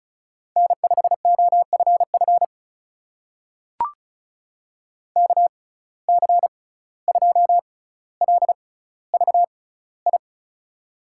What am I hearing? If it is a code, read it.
Morse code, heard as a keyed tone, about 34 wpm, spaced as for 17 wpm, 700 Hz. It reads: N5OFF KC2LVI
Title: USA Call Signs - 6x Character Spacing - 35wpm